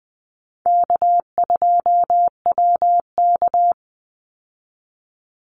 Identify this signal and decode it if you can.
Morse code, heard as a keyed tone, about 20 wpm, 700 Hz. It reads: K2WK
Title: Call Signs - 20wpm